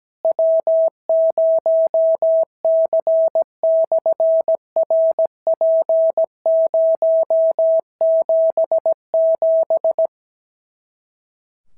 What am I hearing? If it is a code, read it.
Morse code, heard as a keyed tone, about 17 wpm, 650 Hz. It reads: W0C/RP077
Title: SOTA QSO Practice 17wpm